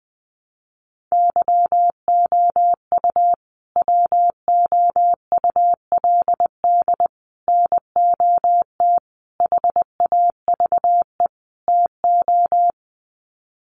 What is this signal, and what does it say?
Morse code, heard as a keyed tone, about 20 wpm, 700 Hz. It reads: YOU WOULD NOT HAVE TO
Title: Sets of 5 Words Encore 20wpm